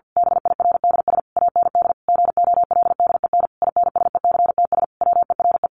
Morse code, heard as a keyed tone, about 50 wpm, 700 Hz. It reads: BIRDS AND GOLDEN INSECTS WERE
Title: A Sound Of Thunder 50wpm